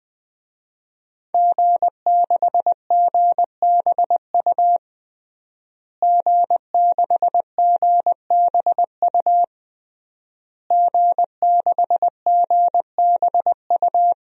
Morse code, heard as a keyed tone, about 20 wpm, 700 Hz. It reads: G6GBU G6GBU G6GBU
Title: DXCC-100 Call Signs - 5 Characters - Repeated 3x - 20wpm